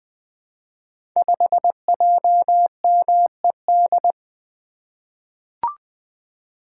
Morse code, heard as a keyed tone, about 20 wpm, 700 Hz. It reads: 5JMED
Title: Code Groups - 5 Letter:Numbers - 20wpm